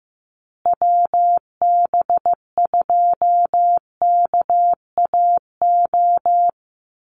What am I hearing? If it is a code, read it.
Morse code, heard as a keyed tone, about 15 wpm, 700 Hz. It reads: WB2KAO
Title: Call Signs Repeated 3x - 15wpm